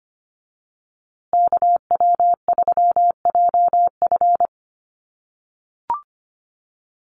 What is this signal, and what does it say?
Morse code, heard as a keyed tone, about 25 wpm, 700 Hz. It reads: KW3JF
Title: USA Call Signs Repeated 3x 25wpm